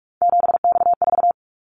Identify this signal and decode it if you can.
Morse code, heard as a keyed tone, about 44 wpm, 700 Hz. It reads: 7X4
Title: Callsign Prefixes for Top 100 DXCC - 45wpm